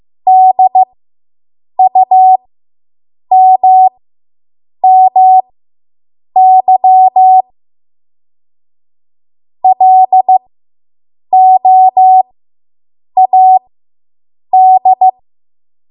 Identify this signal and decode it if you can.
Morse code, heard as a keyed tone, about 15 wpm, spaced as for 7 wpm, 750 Hz. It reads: DUMMY LOAD